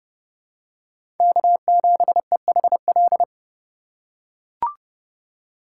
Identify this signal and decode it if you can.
Morse code, heard as a keyed tone, about 30 wpm, 700 Hz. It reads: K7EHL
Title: USA Call Signs - 5 Characters - 4x Character Spacing - 30wpm